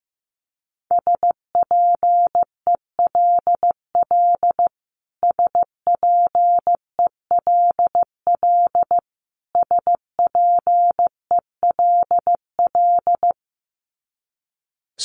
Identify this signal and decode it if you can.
Morse code, heard as a keyed tone, about 14 wpm, 700 Hz. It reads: SPELL SPELL SPELL
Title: Top 500 Words Repeated 15wpm